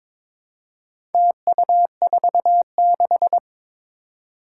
Morse code, heard as a keyed tone, about 22 wpm, 700 Hz. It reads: TU46
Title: Four Challenging Letter-Number Combinations 22wpm